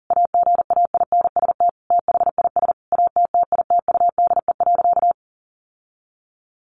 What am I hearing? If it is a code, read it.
Morse code, heard as a keyed tone, about 40 wpm, 700 Hz. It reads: AGAINST THIS ATTITUDE.
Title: Sentences from Top 1300 Words 40wpm